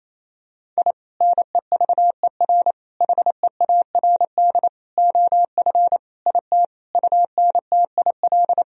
Morse code, heard as a keyed tone, about 28 wpm, 700 Hz. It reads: I NEVER HEARD OF IT UNTIL
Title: Sentences from Top 1000 Words - Review Part 1 - 28wpm